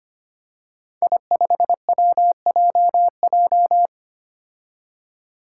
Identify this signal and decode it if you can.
Morse code, heard as a keyed tone, about 25 wpm, 700 Hz. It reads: I5WJJ
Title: Five Challenging Letter-Number Combinations 25wpm